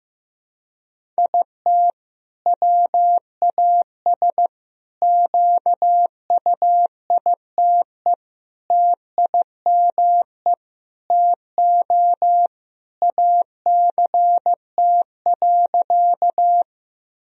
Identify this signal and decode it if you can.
Morse code, heard as a keyed tone, about 16 wpm, 700 Hz. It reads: IT WAS QUITE TIME TO ACT.
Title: Sentences from Top 1000 Words - Review Part 1 - 15wpm